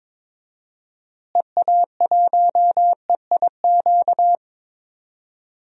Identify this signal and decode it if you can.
Morse code, heard as a keyed tone, about 22 wpm, 700 Hz. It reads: EA1EIQ
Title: DXCC-100 Call Signs - 6 Characters - Repeated 3x - 22wpm